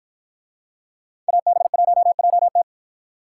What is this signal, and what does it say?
Morse code, heard as a keyed tone, about 53 wpm, 700 Hz. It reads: AB1JT